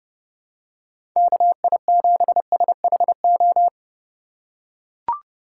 Morse code, heard as a keyed tone, about 30 wpm, 700 Hz. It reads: KI7SHO
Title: USA Call Signs - 6 Characters - 4x Character Spacing - 30wpm